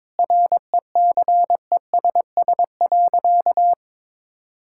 Morse code, heard as a keyed tone, about 22 wpm, 700 Hz. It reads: RECESS.
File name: practice / 22wpm / Easy Sentences with Hints - 22wpm